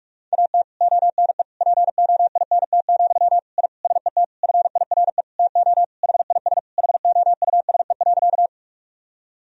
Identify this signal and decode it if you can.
Morse code, heard as a keyed tone, about 41 wpm, 700 Hz. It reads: AT ONE POINT, I SET FIRE TO HIS HOUSE.
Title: Sentences from Top 1000 Words - Review Part 1 - 45wpm